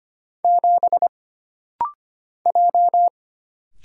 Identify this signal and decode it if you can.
Morse code, heard as a keyed tone, about 25 wpm, 700 Hz. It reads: 7 J
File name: Single Character (Letter:Number:Punctuation:Prosign) - Rapid-Fire - 25wpm